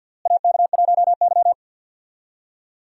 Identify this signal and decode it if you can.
Morse code, heard as a keyed tone, about 52 wpm, 700 Hz. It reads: AK1Y
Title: USA Call Signs - 4 Characters - Repeated 3x - 50wpm